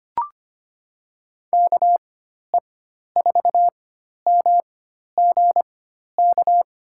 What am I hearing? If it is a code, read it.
Morse code, heard as a keyed tone, about 25 wpm, spaced as for 12 wpm, 700 Hz. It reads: KE4MGK
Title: USA Call Signs - 4x Character Spacing - 25wpm